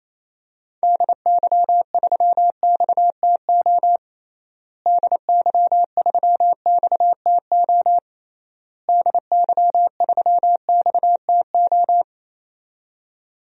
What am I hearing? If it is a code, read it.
Morse code, heard as a keyed tone, about 28 wpm, 700 Hz. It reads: DY3XTO DY3XTO DY3XTO
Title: DXCC-100 Call Signs - 6 Characters - Repeated 3x - 28wpm